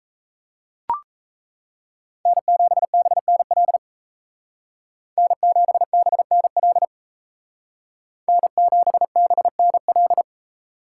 Morse code, heard as a keyed tone, about 35 wpm, 700 Hz. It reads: N7BNL N7BNL N7BNL